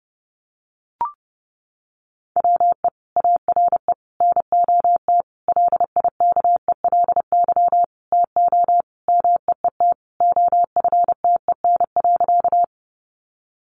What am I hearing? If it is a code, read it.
Morse code, heard as a keyed tone, about 30 wpm, 700 Hz. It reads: WE ARE NOT LIKELY TO MEET OFTEN.
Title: Sentences from Top 1100 Words 30wpm